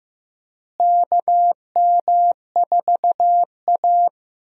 Morse code, heard as a keyed tone, about 16 wpm, 700 Hz. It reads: KM4A
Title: USA Call Signs - 4 Characters - Repeated 2x - 15wpm